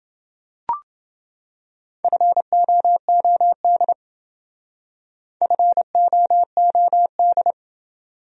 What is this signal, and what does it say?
Morse code, heard as a keyed tone, about 28 wpm, 700 Hz. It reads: FOOD FOOD